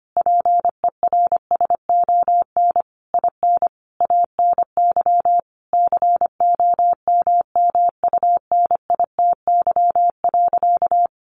Morse code, heard as a keyed tone, about 25 wpm, 700 Hz. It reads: PERSON IN ANY COMMUNITY.
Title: Sentences from Top 1200 Words 25wpm